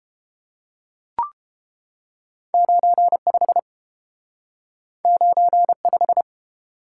Morse code, heard as a keyed tone, about 30 wpm, 700 Hz. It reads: 95 95